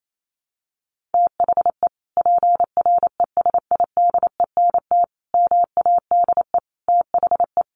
Morse code, heard as a keyed tone, about 28 wpm, 700 Hz. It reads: THE PRESIDENT MADE THE
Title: Sentences from Top 1400 Words 28wpm